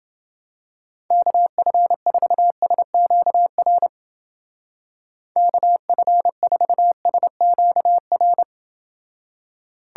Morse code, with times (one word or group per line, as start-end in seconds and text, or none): 1.1-3.9 KF4SQR
5.4-8.4 KF4SQR